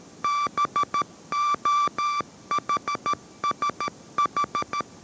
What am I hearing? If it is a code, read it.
Morse code, heard as a keyed tone, about 13 wpm, 1170 Hz. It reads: BOHSH